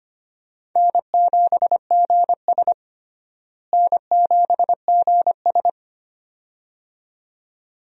Morse code, heard as a keyed tone, about 25 wpm, 700 Hz. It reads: N7GS N7GS
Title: USA Call Signs - 4 Characters - Repeated 2x - 25wpm